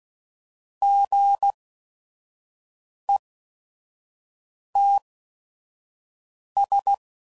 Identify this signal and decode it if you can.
Morse code, heard as a keyed tone, about 16 wpm, spaced as for 5 wpm, 800 Hz. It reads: GETS
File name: Sayings / Chuckles / Squeaky